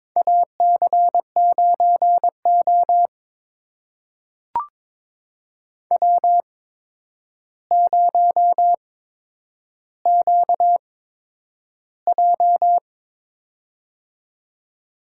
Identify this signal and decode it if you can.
Morse code, heard as a keyed tone, about 22 wpm, spaced as for 9 wpm, 700 Hz. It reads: AC9O W0QJ
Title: USA Call Signs - 4 Characters - 8x Character Spacing - 22wpm